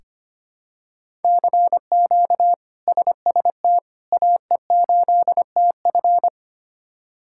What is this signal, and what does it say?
Morse code, heard as a keyed tone, about 25 wpm, 700 Hz. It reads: CQ SST AE8TF
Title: K1USN SST QSO Practice - 25wpm